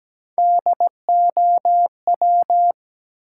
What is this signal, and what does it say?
Morse code, heard as a keyed tone, about 17 wpm, 700 Hz. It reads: DOW
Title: Sets of 5 Words - New Word Drill - 17wpm